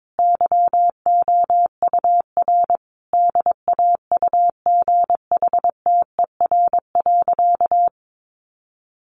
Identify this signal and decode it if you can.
Morse code, heard as a keyed tone, about 22 wpm, 700 Hz. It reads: YOUR DAUGHTER.
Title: Sentences from Top 1500 Words 22wpm